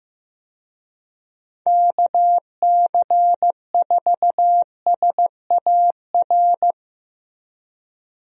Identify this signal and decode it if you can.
Morse code, heard as a keyed tone, about 16 wpm, 700 Hz. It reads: KC4SAR
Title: USA Call Signs - 6 Characters - Repeated 2x - 15wpm